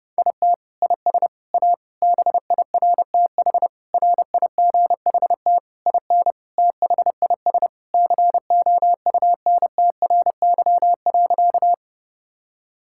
Morse code, heard as a keyed tone, about 30 wpm, 700 Hz. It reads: IT IS A BIRTH RIGHT IN THIS COUNTRY.
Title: Sentences from Top 1000 Words - Review Part 5 - 30wpm